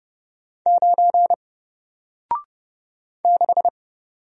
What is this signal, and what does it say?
Morse code, heard as a keyed tone, about 31 wpm, 700 Hz. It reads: 9 6
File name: Single-Letter-Number-30wpm